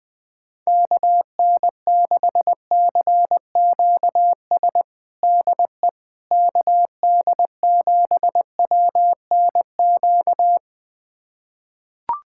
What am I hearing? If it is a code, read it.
Morse code, heard as a keyed tone, about 20 wpm, 700 Hz. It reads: KN6CQS DE KD7WNQ
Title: USA Callsign DE Callsign - 6 Characters - 20wpm